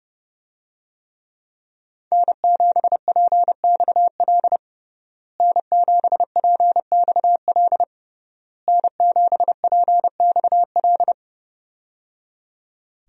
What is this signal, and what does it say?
Morse code, heard as a keyed tone, about 30 wpm, 700 Hz. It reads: N7PXL N7PXL N7PXL
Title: USA Call Signs - 5 Characters - Repeated 3x - 30wpm